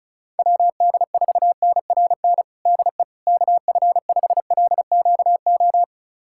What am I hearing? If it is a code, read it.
Morse code, heard as a keyed tone, about 36 wpm, 700 Hz. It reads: WD4NRN DE KF5LQO
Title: USA Callsign DE Callsign - 6 Characters - 35wpm